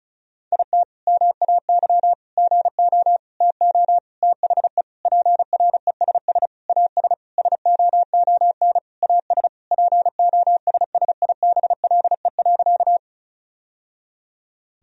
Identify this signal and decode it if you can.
Morse code, heard as a keyed tone, about 36 wpm, 700 Hz. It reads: IT MAY GO TO THE PRESS AS SOON AS POSSIBLE.
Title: Sentences from Top 1000 Words - Review Part 1 - 35wpm